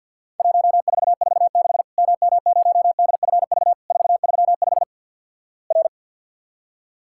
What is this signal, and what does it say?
Morse code, heard as a keyed tone, about 50 wpm, 700 Hz. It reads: 1446 MM0DFV 435 R
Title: Worked All Europe DX Contest - QTC Exchange - 50wpm